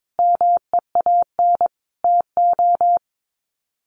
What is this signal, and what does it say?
Morse code, heard as a keyed tone, about 22 wpm, 700 Hz. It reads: MEAN TO
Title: Sets of 5 Words Encore2 22wpm